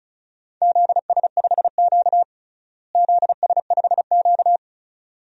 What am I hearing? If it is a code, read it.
Morse code, heard as a keyed tone, about 36 wpm, 700 Hz. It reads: ZS5Q ZS5Q
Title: DXCC-100 Call Signs - 4 Characters - Repeated 2x - 35wpm